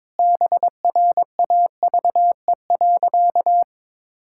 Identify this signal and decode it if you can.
Morse code, heard as a keyed tone, about 22 wpm, 700 Hz. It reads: BRAVE.
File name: Sentences with 9-Word Phrases - 22wpm